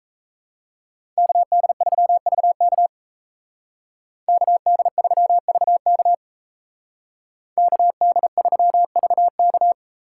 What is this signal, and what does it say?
Morse code, heard as a keyed tone, about 34 wpm, 700 Hz. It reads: KD3VK KD3VK KD3VK
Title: Speed Racing - USA 2x2 Call Signs 28wpm